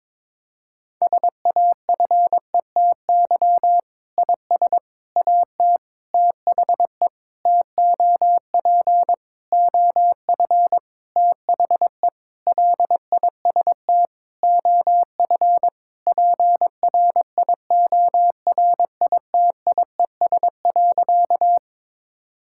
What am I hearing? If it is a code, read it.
Morse code, heard as a keyed tone, about 22 wpm, 700 Hz. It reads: SAFETY IS AT THE TOP OF THE LIST OF PRIORITIES.
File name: Sentences with 7-Word Phrases - 22wpm